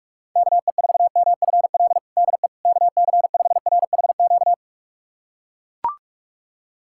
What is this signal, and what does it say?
Morse code, heard as a keyed tone, about 41 wpm, 700 Hz. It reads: KE4MFL DE KC5RHQ
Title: USA Callsign DE Callsign 45wpm